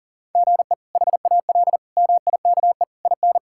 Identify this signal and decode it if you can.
Morse code, heard as a keyed tone, about 40 wpm, 700 Hz. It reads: GE HAL MIKE IN
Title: K1USN SST QSO Practice - 40wpm